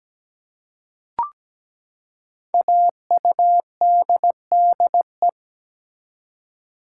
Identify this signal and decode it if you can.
Morse code, heard as a keyed tone, about 17 wpm, 700 Hz. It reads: AUDDE
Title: Code Groups - 5 Letters - 17wpm